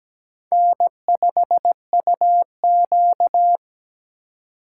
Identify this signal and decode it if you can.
Morse code, heard as a keyed tone, about 20 wpm, 700 Hz. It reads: N5UQ
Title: Speed Racing - USA 1x2 Call Signs 17wpm